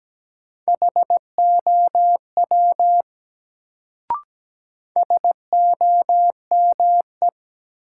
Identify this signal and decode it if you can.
Morse code, heard as a keyed tone, about 17 wpm, 700 Hz. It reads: HOW SOME
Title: Sentences from Top 100 Words - New Word Drill - 17wpm